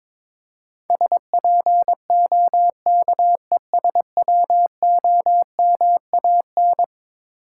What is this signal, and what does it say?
Morse code, heard as a keyed tone, about 22 wpm, 700 Hz. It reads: SPOKESWOMAN
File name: practice / 22wpm / Eleven Letter Words - 22wpm